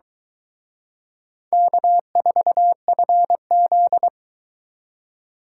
Morse code, heard as a keyed tone, about 23 wpm, 700 Hz. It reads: K4FZ